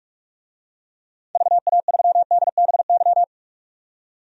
Morse code, heard as a keyed tone, about 46 wpm, 700 Hz. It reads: VA3DBY
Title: DXCC-100 Call Signs - 6 Characters - 45wpm